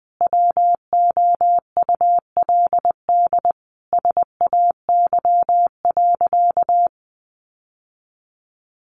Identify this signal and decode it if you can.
Morse code, heard as a keyed tone, about 19 wpm, 700 Hz. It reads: WOULD SAY.
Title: Sentences from Top 1500 Words 20wpm